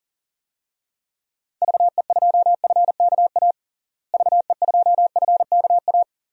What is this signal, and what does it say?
Morse code, heard as a keyed tone, about 37 wpm, 700 Hz. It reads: VE2FKA VE2FKA
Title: DXCC-100 Call Signs - 6 Characters - Repeated 3x - 40wpm